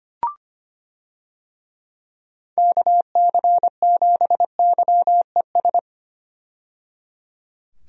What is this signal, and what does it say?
Morse code, heard as a keyed tone, about 25 wpm, 700 Hz. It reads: KC7YES
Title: USA Call Signs - 6 Characters - 25wpm